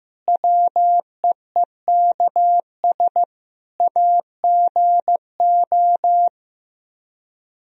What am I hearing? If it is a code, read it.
Morse code, heard as a keyed tone, about 15 wpm, 700 Hz. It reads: WEEKS AGO
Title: Speed-Racing - Sets of 5 Words 15wpm